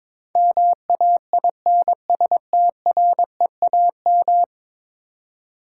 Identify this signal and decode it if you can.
Morse code, heard as a keyed tone, about 22 wpm, 700 Hz. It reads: MAINSTREAM
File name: practice / 22wpm / 10 Letter Words - First 3 Letters Given - 22wpm